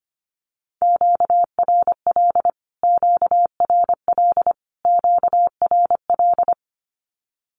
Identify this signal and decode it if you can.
Morse code, heard as a keyed tone, about 25 wpm, 700 Hz. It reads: QRL QRL QRL
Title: Q-Codes Repeated 3x 25wpm